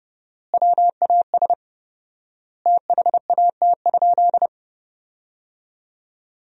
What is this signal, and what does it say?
Morse code, heard as a keyed tone, about 30 wpm, 700 Hz. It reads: WAS THAT?
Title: 4x Word-Spacing - Sentences from Top 100 Words - 30wpm